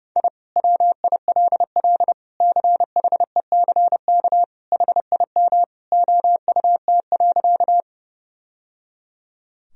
Morse code, heard as a keyed tone, about 30 wpm, 700 Hz. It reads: I WILL CHECK HIM OUT.
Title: Sentences from Top 1000 Words - Review Part 3 - 30wpm